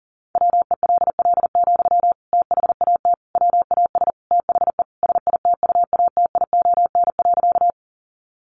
Morse code, heard as a keyed tone, about 40 wpm, 700 Hz. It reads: WELL, THAT WAS THE SITUATION.
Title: Sentences from Top 1200 Words 40wpm